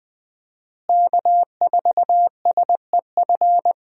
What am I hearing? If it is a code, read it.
Morse code, heard as a keyed tone, about 21 wpm, 700 Hz. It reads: K4SEF
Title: USA Call Signs - 5 Characters - Repeated 2x - 20wpm